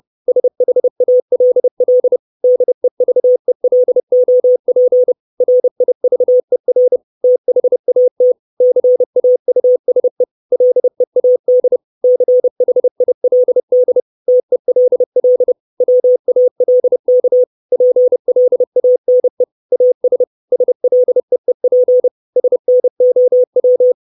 Morse code, heard as a keyed tone, about 30 wpm, 500 Hz. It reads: SHALL DEVELOP RIVER THAT CAUSE LEAD CHILD TELL WALK PLANE AS SLEEP SNOW
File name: 632 Words 30WPM